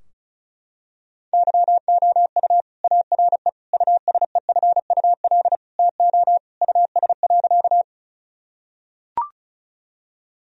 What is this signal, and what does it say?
Morse code, heard as a keyed tone, about 34 wpm, 700 Hz. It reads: YOU ARE USEFUL TO US.
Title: Sentences from Top 1000 Words - Review Part 4 - 35wpm